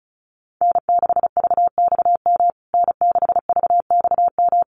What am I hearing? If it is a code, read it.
Morse code, heard as a keyed tone, about 34 wpm, 700 Hz. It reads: N6VXM N6VXM
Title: USA Call Signs Repeated 3x 35wpm